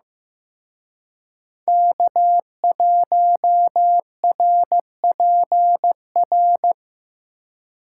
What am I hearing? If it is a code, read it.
Morse code, heard as a keyed tone, about 18 wpm, 700 Hz. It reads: K1RPR
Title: Speed Racing - USA Call Signs 15wpm